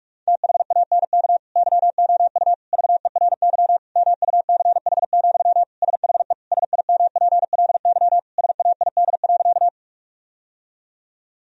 Morse code, heard as a keyed tone, about 46 wpm, 700 Hz. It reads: THANK YOU VERY MUCH, SHE SIMPLY SAID.
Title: Sentences from Top 1000 Words - Review Part 4 - 45wpm